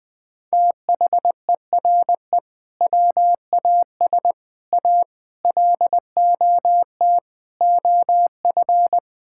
Morse code, heard as a keyed tone, about 20 wpm, 700 Hz. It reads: THERE WAS A LOT OF
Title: Build A Phrase - 5-Word Phrases - 20wpm